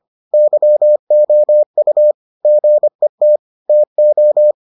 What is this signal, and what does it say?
Morse code, heard as a keyed tone, about 25 wpm, 600 Hz. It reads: YOU GET TO